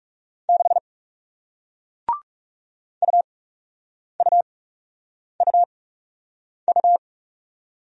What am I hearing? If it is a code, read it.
Morse code, heard as a keyed tone, about 40 wpm, spaced as for 8 wpm, 700 Hz. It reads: 6 UUUU